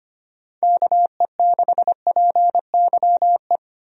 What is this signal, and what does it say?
Morse code, heard as a keyed tone, about 25 wpm, 700 Hz. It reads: KE6PYE
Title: USA Call Signs - 6 Characters - 25wpm